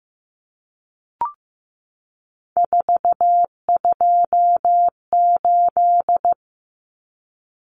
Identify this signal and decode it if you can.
Morse code, heard as a keyed tone, about 15 wpm, 700 Hz. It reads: 428
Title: Common Cut Numbers - 3-Digit 15wpm